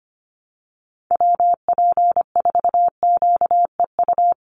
Morse code, heard as a keyed tone, about 25 wpm, 700 Hz. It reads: WP4QEU
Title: USA Call Signs Repeated 3x 25wpm